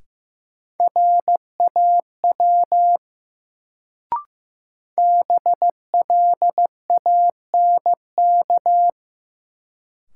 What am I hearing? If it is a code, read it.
Morse code, heard as a keyed tone, about 15 wpm, 700 Hz. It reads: RAW BLANK
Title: Sentences from Top 1500 Words - New Word Drill Alt - 15wpm